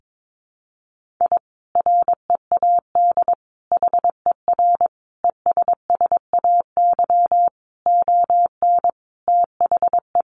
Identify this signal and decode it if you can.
Morse code, heard as a keyed tone, about 22 wpm, 700 Hz. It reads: I READ HER ESSAY ON THE
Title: Sentences from Top 1500 Words 22wpm